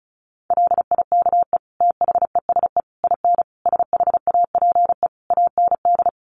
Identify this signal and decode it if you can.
Morse code, heard as a keyed tone, about 34 wpm, 700 Hz. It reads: LIKE THESE IN SHAPE AND
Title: Sentences from Top 500 Words 35wpm